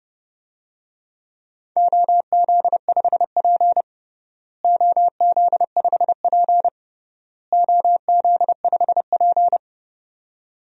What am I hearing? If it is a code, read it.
Morse code, heard as a keyed tone, about 31 wpm, 700 Hz. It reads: OZ5P OZ5P OZ5P
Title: DXCC-100 Call Signs - 4 Characters - Repeated 3x - 30wpm